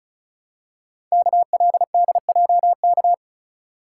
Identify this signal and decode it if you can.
Morse code, heard as a keyed tone, about 36 wpm, 700 Hz. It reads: KLDJK
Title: Code Groups - 5 Letters - 35wpm